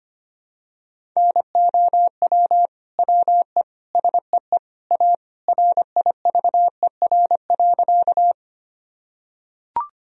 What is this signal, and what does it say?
Morse code, heard as a keyed tone, about 25 wpm, 700 Hz. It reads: NOW WE SEE A RIVER.
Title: Sentences from Top 1000 Words - Review Part 1 - 25wpm